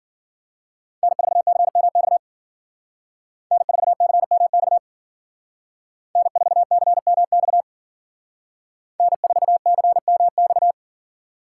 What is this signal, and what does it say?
Morse code, heard as a keyed tone, about 49 wpm, 700 Hz. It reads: N4CMX N4CMX N4CMX N4CMX
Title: Speed Racing - USA 1x3 Call Signs 40wpm